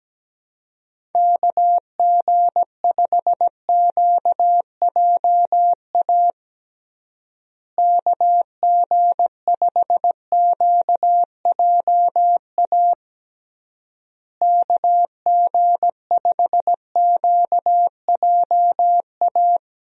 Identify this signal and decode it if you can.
Morse code, heard as a keyed tone, about 17 wpm, 700 Hz. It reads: KG5QJA KG5QJA KG5QJA
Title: USA Call Signs - 6 Characters - Repeated 3x - 17wpm